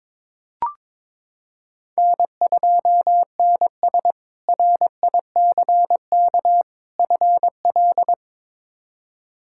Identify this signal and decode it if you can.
Morse code, heard as a keyed tone, about 22 wpm, 700 Hz. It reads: N2NS RICK FL
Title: K1USN SST QSO Practice - 22wpm